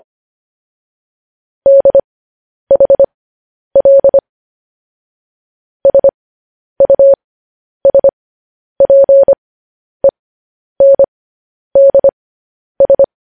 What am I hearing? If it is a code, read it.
Morse code, heard as a keyed tone, about 25 wpm, spaced as for 10 wpm, 550 Hz. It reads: DHL SUSPENDS
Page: News Headlines in Morse Code at 10 WPM